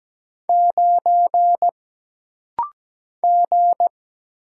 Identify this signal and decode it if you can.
Morse code, heard as a keyed tone, about 17 wpm, 700 Hz. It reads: 9 G
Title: Single Letter-Number Alt - Rapid-Fire 17wpm